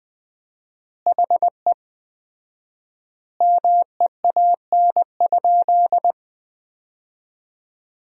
Morse code, heard as a keyed tone, about 20 wpm, 700 Hz. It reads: HE MEAN?
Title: 4x Word-Spacing - Sentences from Top 200 Words - 20wpm